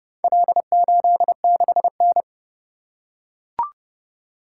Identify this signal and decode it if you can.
Morse code, heard as a keyed tone, about 30 wpm, 700 Hz. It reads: L86N